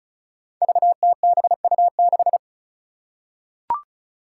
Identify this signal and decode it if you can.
Morse code, heard as a keyed tone, about 34 wpm, 700 Hz. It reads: VTBU6
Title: Five Challenging Letter-Number Combinations 35wpm